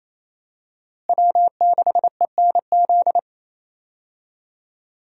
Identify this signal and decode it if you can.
Morse code, heard as a keyed tone, about 28 wpm, 700 Hz. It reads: W6ENZ
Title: USA Call Signs - 5 Characters - Repeated 3x - 28wpm